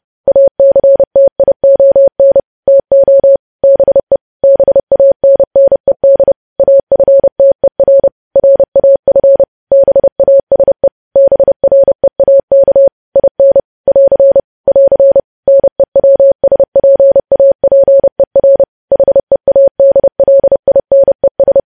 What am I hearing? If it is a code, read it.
Morse code, heard as a keyed tone, about 30 wpm, 550 Hz. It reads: ACTION TO BE BANNED AFTER RAF BASE BREAK IN <AR> <AR> NEWSPAPER HEADLINES
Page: news headlines in morse code at 30 wpm